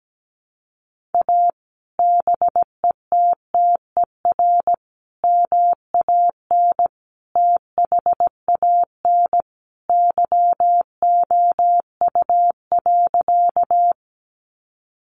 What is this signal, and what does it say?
Morse code, heard as a keyed tone, about 17 wpm, 700 Hz. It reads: A BETTER MAN THAN YOU.
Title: Sentences from Top 400 Words 17wpm